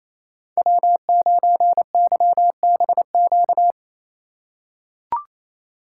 Morse code, heard as a keyed tone, about 28 wpm, 700 Hz. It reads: W9YBQ
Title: USA Call Signs - 2x Character Spacing - 28wpm